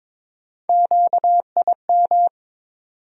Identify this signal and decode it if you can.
Morse code, heard as a keyed tone, about 22 wpm, 700 Hz. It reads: QIM
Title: USA Call Sign Postfixes - 3 Letters 22wpm